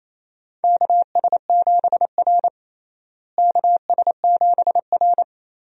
Morse code, heard as a keyed tone, about 28 wpm, 700 Hz. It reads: KS7R KS7R
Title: USA Call Signs - 4 Characters - Repeated 3x - 28wpm